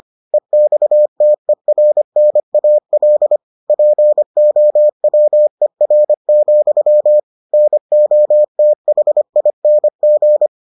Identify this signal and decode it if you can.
Morse code, heard as a keyed tone, about 25 wpm, 600 Hz. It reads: EXTERNAL POWER, NOTHING